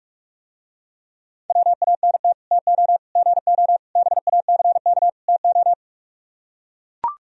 Morse code, heard as a keyed tone, about 44 wpm, 700 Hz. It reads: WANT TO GO BACK TO